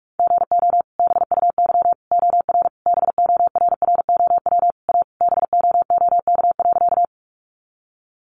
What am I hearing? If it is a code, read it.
Morse code, heard as a keyed tone, about 44 wpm, 700 Hz. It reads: GO BUY OR BORROW A BOOK.
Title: Sentences from Top 1500 Words 45wpm